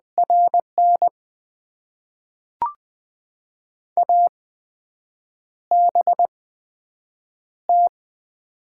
Morse code, heard as a keyed tone, about 20 wpm, spaced as for 7 wpm, 700 Hz. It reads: RN ABT
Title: Top 90 QSO Elements - 8x Character Spacing - 20wpm